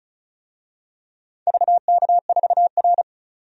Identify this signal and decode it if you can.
Morse code, heard as a keyed tone, about 36 wpm, 700 Hz. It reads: VK4R
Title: DXCC-100 Call Signs - 4 Characters - 35wpm